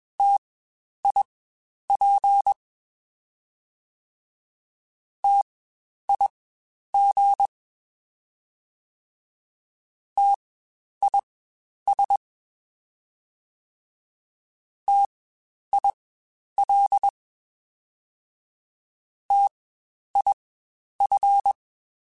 Morse code, heard as a keyed tone, about 21 wpm, spaced as for 10 wpm, 800 Hz. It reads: TIP TIG TIS TIL TIF
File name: Spaced Alphas / Spaced Alphas mp3s / Ts sp @20